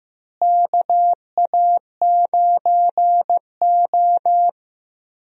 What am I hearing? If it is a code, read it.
Morse code, heard as a keyed tone, about 15 wpm, 700 Hz. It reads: KA9O
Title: USA Call Signs - 4 Characters - Repeated 2x - 15wpm